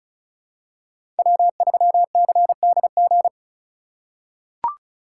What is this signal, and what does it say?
Morse code, heard as a keyed tone, about 36 wpm, 700 Hz. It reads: W3CDG
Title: USA Call Signs - 2x Character Spacing - 35wpm